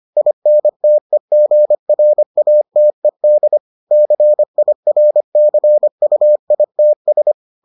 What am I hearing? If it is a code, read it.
Morse code, heard as a keyed tone, about 25 wpm, 600 Hz. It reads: INTEGRATED CIRCUITS